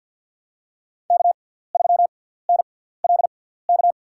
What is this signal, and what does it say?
Morse code, heard as a keyed tone, about 51 wpm, spaced as for 17 wpm, 700 Hz. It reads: K3NLK
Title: USA Call Signs - 5 Characters - 6x Character Spacing - 50wpm